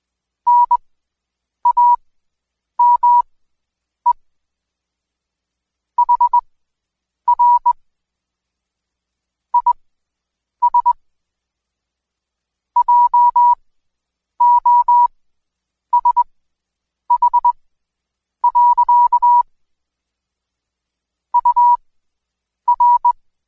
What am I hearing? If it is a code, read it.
Morse code, heard as a keyed tone, about 21 wpm, spaced as for 8 wpm, 970 Hz. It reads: NAME HR IS JOSH. UR
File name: Code Quick / AudioDownld2 / Seg114-Day19-DayDrill 19